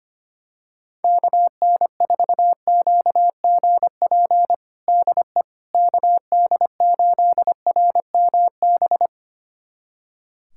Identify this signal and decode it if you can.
Morse code, heard as a keyed tone, about 25 wpm, 700 Hz. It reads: KN4QGP DE KD8RMB
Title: USA Callsign DE Callsign 25wpm